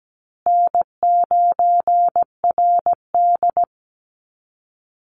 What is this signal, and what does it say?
Morse code, heard as a keyed tone, about 17 wpm, 700 Hz. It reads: N9RD
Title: Call Signs Repeated 2x - 17wpm